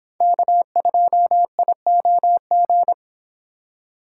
Morse code, heard as a keyed tone, about 25 wpm, 700 Hz. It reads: K2IOG